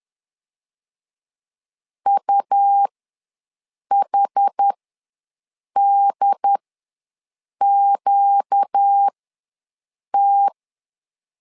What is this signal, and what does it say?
Morse code, heard as a keyed tone, about 10 wpm, spaced as for 6 wpm, 800 Hz. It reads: UHDQT
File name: LEZIONI / 4 Q ws 10wpm 2306